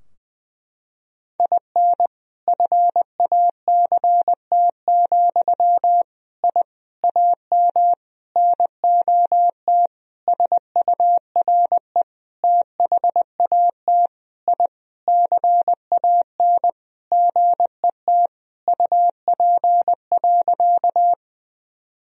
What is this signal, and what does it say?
Morse code, heard as a keyed tone, about 20 wpm, 700 Hz. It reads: IN FACT, I AM NOT SURE THAT I CAN GET UP.
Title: Sentences from Top 1000 Words - Review Part 3 - 20wpm